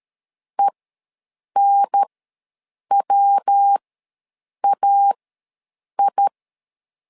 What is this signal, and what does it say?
Morse code, heard as a keyed tone, about 13 wpm, spaced as for 7 wpm, 800 Hz. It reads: ENWAI
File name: LEZIONI / 3 MWRO / 3 MW ws 12wpm 1816